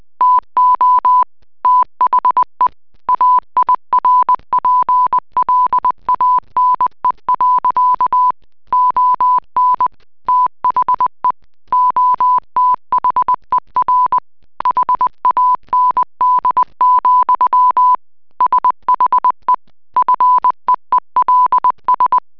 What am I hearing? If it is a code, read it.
Morse code, heard as a keyed tone, about 20 wpm, 1000 Hz. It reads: TO THE AIRPLANE. ON THE OTHER HAND, SHE FEELS